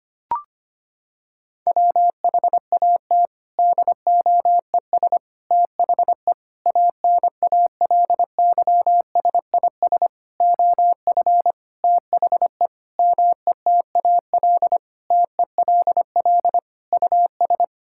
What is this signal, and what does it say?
Morse code, heard as a keyed tone, about 25 wpm, 700 Hz. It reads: WHAT DOES THE ANALYSIS OF THE METAL TELL US
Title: Sentences from Top 1000 Words - Review Part 6 - 25wpm